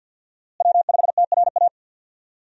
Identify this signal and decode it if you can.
Morse code, heard as a keyed tone, about 53 wpm, 700 Hz. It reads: W5TRA
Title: USA Call Signs - 5 Characters - Repeated 2x - 50wpm